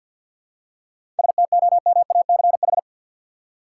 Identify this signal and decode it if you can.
Morse code, heard as a keyed tone, about 48 wpm, 700 Hz. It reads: STOMACH
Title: Seven Letter Words - 50wpm